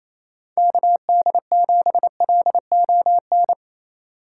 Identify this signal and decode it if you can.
Morse code, heard as a keyed tone, about 28 wpm, 700 Hz. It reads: KD7LON
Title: USA Call Signs - 6 Characters - 4x Character Spacing - 28wpm